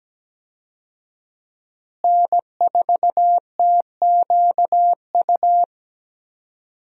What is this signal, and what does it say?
Morse code, heard as a keyed tone, about 17 wpm, 700 Hz. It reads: N4TQU
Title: USA Call Signs - 5 Characters - Repeated 2x - 17wpm